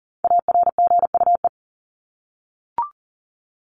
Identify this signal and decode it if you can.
Morse code, heard as a keyed tone, about 40 wpm, 700 Hz. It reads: ARGUE
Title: Top 1501-1750 Words 40wpm